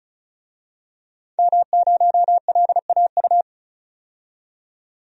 Morse code, heard as a keyed tone, about 36 wpm, 700 Hz. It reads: M0LAU
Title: DXCC-100 Call Signs - 5 Characters - 35wpm